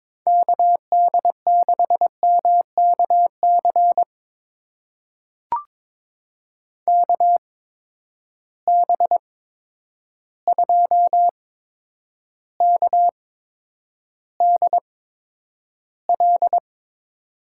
Morse code, heard as a keyed tone, about 21 wpm, spaced as for 9 wpm, 700 Hz. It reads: KD6MKC KB2KDL
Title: USA Call Signs - 6 Characters - 8x Character Spacing - 22wpm